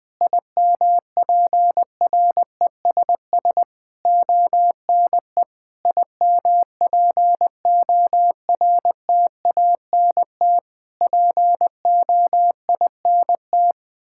Morse code, heard as a keyed tone, about 20 wpm, 700 Hz. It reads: IMPRESS ONE IMPORTANT POINT
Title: Sentences from Top 1600 Words 20wpm